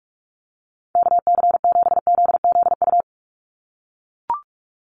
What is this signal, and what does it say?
Morse code, heard as a keyed tone, about 44 wpm, 700 Hz. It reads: KC7ZZU
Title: USA Call Signs 45wpm